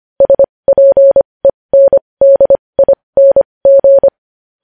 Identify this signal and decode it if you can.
Morse code, heard as a keyed tone, about 25 wpm, spaced as for 20 wpm, 550 Hz. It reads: SPENDING
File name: podcast / News headlines-1x-25x20